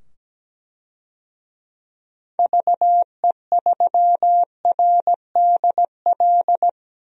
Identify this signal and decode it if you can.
Morse code, heard as a keyed tone, about 17 wpm, 700 Hz. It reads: VE3RDL
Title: DXCC-100 Call Signs - 6 Characters - 17wpm